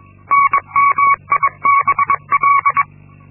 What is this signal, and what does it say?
Morse code, heard as a keyed tone, about 22 wpm, 1120 Hz. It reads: NMIBL
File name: Linksmakalnis-2